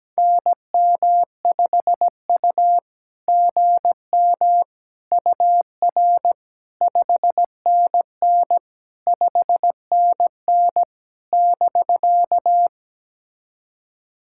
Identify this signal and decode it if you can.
Morse code, heard as a keyed tone, about 17 wpm, 690 Hz. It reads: NM5U GM UR 5NN 5NN <BK>
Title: SOTA QSO Practice 17wpm